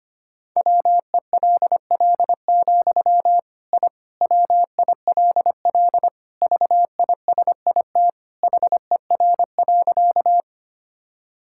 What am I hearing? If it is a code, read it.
Morse code, heard as a keyed tone, about 25 wpm, 700 Hz. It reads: WELL, I WILL VISIT HER.
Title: Sentences from Top 1000 Words - Review Part 6 - 25wpm